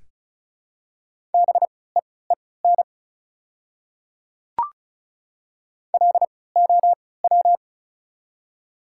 Morse code, heard as a keyed tone, about 36 wpm, spaced as for 20 wpm, 700 Hz. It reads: BEEN LOW
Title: 3x Character Spacing - Top 200 Words - 35wpm